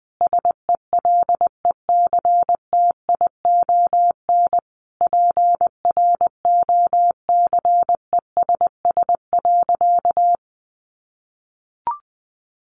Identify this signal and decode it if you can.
Morse code, heard as a keyed tone, about 20 wpm, 700 Hz. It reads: SELECTION PROCESS.
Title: Sentences from Top 1400 Words 20wpm